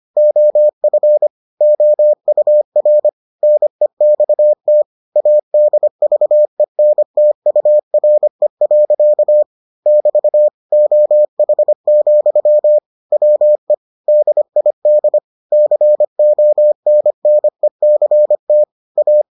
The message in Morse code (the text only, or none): OF OUR NEXT ADVENTURE. <BT> OH, WE DID CONNECT A